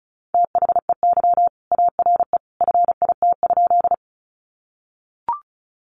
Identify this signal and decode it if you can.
Morse code, heard as a keyed tone, about 34 wpm, 700 Hz. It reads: THEY ARE FIT?
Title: Sentences from Top 800 Words 35wpm